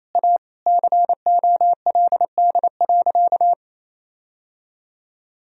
Morse code, heard as a keyed tone, about 28 wpm, 700 Hz. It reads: A COLD.
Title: Sentences from Top 1000 Words - Review Part 3 - 28wpm